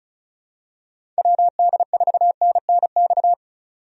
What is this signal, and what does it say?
Morse code, heard as a keyed tone, about 36 wpm, 700 Hz. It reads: WD4NNX
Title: USA Call Signs - 6 Characters - 35wpm